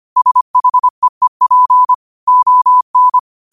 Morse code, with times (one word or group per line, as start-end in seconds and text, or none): 0.2-1.9 SHEEP
2.3-3.2 ON